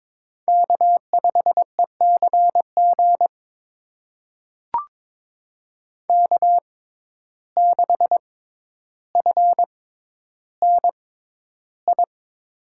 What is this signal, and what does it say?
Morse code, heard as a keyed tone, about 22 wpm, spaced as for 11 wpm, 700 Hz. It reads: K5ECG K6FNI
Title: USA Call Signs - 5 Characters - 6x Character Spacing - 22wpm